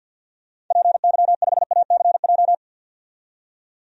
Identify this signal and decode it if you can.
Morse code, heard as a keyed tone, about 51 wpm, 700 Hz. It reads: PY5ACJ
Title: DXCC-100 Call Signs - 6 Characters - 50wpm